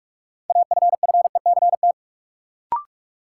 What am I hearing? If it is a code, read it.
Morse code, heard as a keyed tone, about 42 wpm, 700 Hz. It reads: AFFECT
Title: Sentences from Top 900 Words - New Word Drill - 45wpm